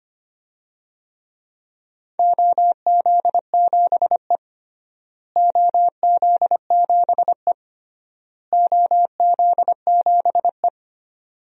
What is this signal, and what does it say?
Morse code, heard as a keyed tone, about 25 wpm, 700 Hz. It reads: OZ7E OZ7E OZ7E
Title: DXCC-100 Call Signs - 4 Characters - Repeated 3x - 25wpm